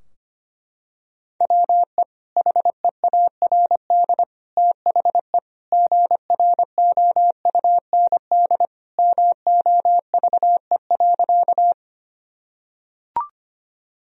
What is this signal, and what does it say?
Morse code, heard as a keyed tone, about 25 wpm, 700 Hz. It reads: WE HEARD THE GROUND MOVE.
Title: Sentences from Top 1000 Words - Review Part 2 - 25wpm